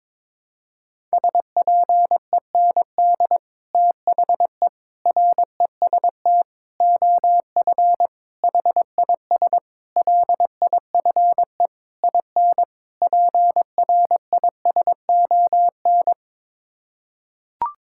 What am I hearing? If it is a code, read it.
Morse code, heard as a keyed tone, about 22 wpm, 700 Hz. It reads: SPEND THE REST OF HIS LIFE IN PRISON
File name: practice / 22wpm / Sets of 8 Words - 22wpm